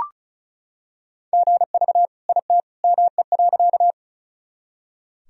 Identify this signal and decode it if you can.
Morse code, heard as a keyed tone, about 36 wpm, 700 Hz. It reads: GV IT ME.
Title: Sentences from Top 200 Words and Abbreviations 35wpm